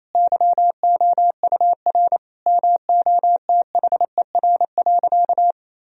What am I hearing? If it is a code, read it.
Morse code, heard as a keyed tone, about 28 wpm, 700 Hz. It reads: YOUR MOTHER.
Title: Sentences from Top 1000 Words - Review Part 3 - 28wpm